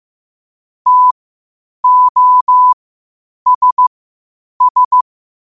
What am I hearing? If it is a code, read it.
Morse code, heard as a keyed tone, about 15 wpm, spaced as for 8 wpm, 1000 Hz. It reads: TOSS
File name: qotdfiles / 8wpm